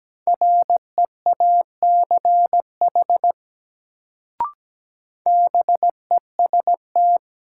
Morse code, heard as a keyed tone, about 17 wpm, 700 Hz. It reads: REACH BEST
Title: Sentences from Top 400 Words - New Word Drill - 17wpm